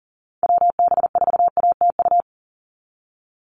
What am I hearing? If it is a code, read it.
Morse code, heard as a keyed tone, about 40 wpm, 690 Hz. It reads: WB4ATU
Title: USA Call Signs 40wpm